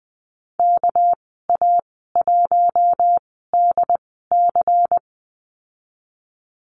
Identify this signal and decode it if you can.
Morse code, heard as a keyed tone, about 20 wpm, spaced as for 14 wpm, 700 Hz. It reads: KA1DC
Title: 200 Lesson 32g - Letter k - Call Signs 20wpm